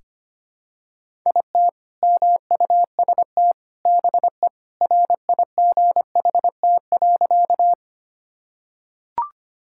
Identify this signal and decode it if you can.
Morse code, heard as a keyed tone, about 25 wpm, 700 Hz. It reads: IT MUST BE RIGHT.
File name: practice / 25wpm / Sentences with 2-Word Phrases - 25wpm